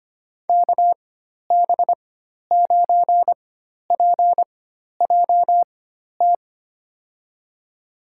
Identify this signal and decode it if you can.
Morse code, heard as a keyed tone, about 25 wpm, spaced as for 12 wpm, 700 Hz. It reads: KB9PJT
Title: USA Call Signs - 6 Characters - 4x Character Spacing - 25wpm